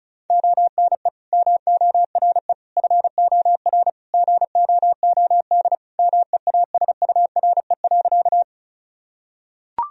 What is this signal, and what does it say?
Morse code, heard as a keyed tone, about 34 wpm, 700 Hz. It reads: ONE MORE FOR GOOD MEASURE.
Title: Sentences from Top 1000 Words - Review Part 2 - 35wpm